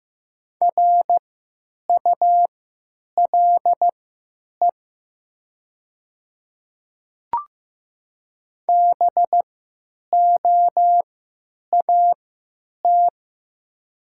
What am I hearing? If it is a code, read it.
Morse code, heard as a keyed tone, about 15 wpm, spaced as for 9 wpm, 700 Hz. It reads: RULE BOAT
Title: 3x Character Spacing - Top 500 Words - 15wpm